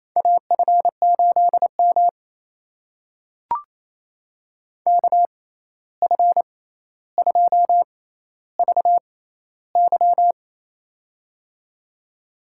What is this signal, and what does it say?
Morse code, heard as a keyed tone, about 28 wpm, spaced as for 13 wpm, 700 Hz. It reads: AF8M KF2VY
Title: USA Call Signs - 6x Character Spacing - 28wpm